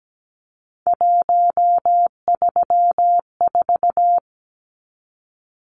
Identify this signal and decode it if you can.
Morse code, heard as a keyed tone, about 17 wpm, 700 Hz. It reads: 134
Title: Numbers - 3-Digit 17wpm